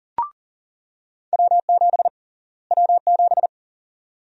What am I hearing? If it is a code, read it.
Morse code, heard as a keyed tone, about 40 wpm, 700 Hz. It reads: W7 W7
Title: USA Call Sign Prefixes - Repeated 2x - 40wpm